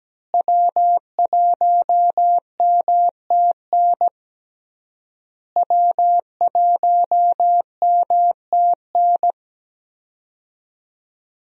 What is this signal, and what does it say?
Morse code, heard as a keyed tone, about 17 wpm, 700 Hz. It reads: W1MTN W1MTN
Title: USA Call Signs - 5 Characters - Repeated 3x - 17wpm